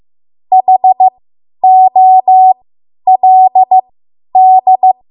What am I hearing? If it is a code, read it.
Morse code, heard as a keyed tone, about 15 wpm, spaced as for 10 wpm, 750 Hz. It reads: HOLD
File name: Morse / 10WPM